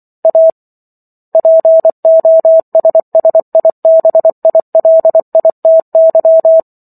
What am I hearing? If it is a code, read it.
Morse code, heard as a keyed tone, about 24 wpm, 650 Hz. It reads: A POSSIBILITY